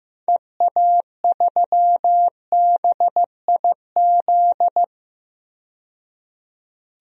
Morse code, heard as a keyed tone, about 15 wpm, 700 Hz. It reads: EA3BIZ
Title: DXCC-100 Call Signs - 6 Characters - 15wpm